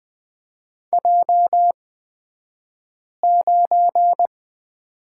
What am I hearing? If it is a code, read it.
Morse code, heard as a keyed tone, about 20 wpm, spaced as for 5 wpm, 700 Hz. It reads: J9
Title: 242 Lesson 39d - Letter z - Two Characters 20wpm